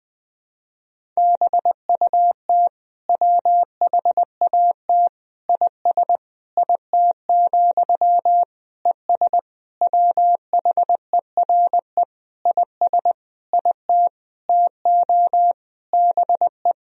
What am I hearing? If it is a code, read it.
Morse code, heard as a keyed tone, about 20 wpm, 700 Hz. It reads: BUT WHAT IS IT, ES WHERE IS IT TO BE
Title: Sentences from Top 300 Words and Abbreviations 20wpm